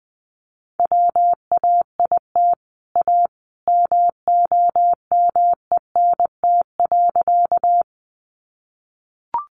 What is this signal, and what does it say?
Morse code, heard as a keyed tone, about 20 wpm, 700 Hz. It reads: WAIT A MOMENT.
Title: Sentences from Top 1000 Words 20wpm